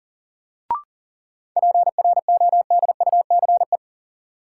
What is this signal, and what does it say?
Morse code, heard as a keyed tone, about 40 wpm, 700 Hz. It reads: PRODUCE
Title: Sentences from Top 1000 Words - Review Part 3 - New Word Drill - 40wpm